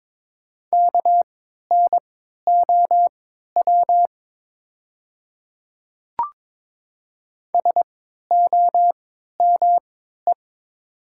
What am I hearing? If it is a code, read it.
Morse code, heard as a keyed tone, about 22 wpm, spaced as for 12 wpm, 700 Hz. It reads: KNOW SOME
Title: 3x Character Spacing - Top 100 Words - 22wpm